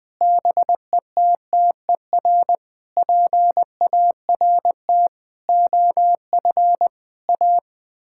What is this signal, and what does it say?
Morse code, heard as a keyed tone, about 20 wpm, 700 Hz. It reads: BETTER PART OF A
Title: Sets of 6 Words - 20wpm